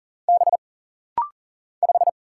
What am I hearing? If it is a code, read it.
Morse code, heard as a keyed tone, about 40 wpm, 700 Hz. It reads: B 5
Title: Single Letter-Number Alt - Rapid-Fire 40wpm